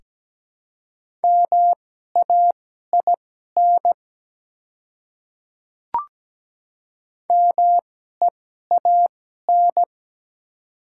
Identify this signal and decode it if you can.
Morse code, heard as a keyed tone, about 17 wpm, spaced as for 12 wpm, 700 Hz. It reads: MAIN MEAN
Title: 2x Character Spacing - Top 300 Words - 17wpm